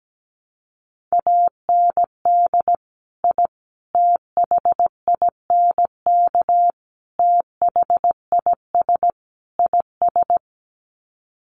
Morse code, heard as a keyed tone, about 17 wpm, 700 Hz. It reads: AND I THINK THIS IS
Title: Sets of 5 Words Encore 17wpm